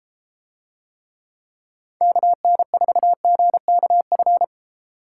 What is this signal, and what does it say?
Morse code, heard as a keyed tone, about 33 wpm, 700 Hz. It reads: KN4GKF